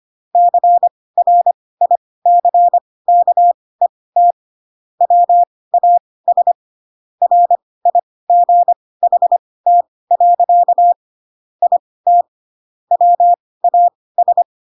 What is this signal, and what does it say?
Morse code, heard as a keyed tone, about 25 wpm, spaced as for 18 wpm, 700 Hz. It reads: CRICKET WAS RIGHT. IT WAS
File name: Pinocchio / Pinocchio ch5 12 25 ik1hge